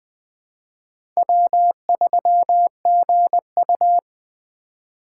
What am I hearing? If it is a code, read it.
Morse code, heard as a keyed tone, about 20 wpm, 700 Hz. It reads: W3GU
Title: USA Call Signs - 3x Character Spacing - 20wpm